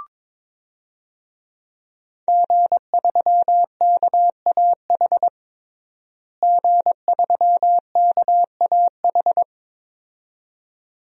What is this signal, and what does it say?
Morse code, heard as a keyed tone, about 21 wpm, 700 Hz. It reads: G3KAH G3KAH
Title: DXCC-100 Call Signs - 5 Characters - Repeated 2x - 22wpm